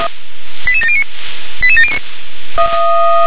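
Five-tone selective call (ZVEI), heard as quick runs of five tones, tones of about 70 ms. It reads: 88709 88709
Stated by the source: ZVEI five-tone call 88690 received via SDR.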